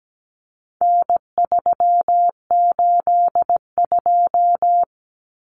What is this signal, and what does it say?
Morse code, heard as a keyed tone, about 17 wpm, 700 Hz. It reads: N382
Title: Common Cut Numbers - 4-Digit 17wpm